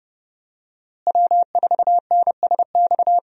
Morse code, heard as a keyed tone, about 30 wpm, 700 Hz. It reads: W4NSX
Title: USA Call Signs - 5 Characters - Repeated 3x - 30wpm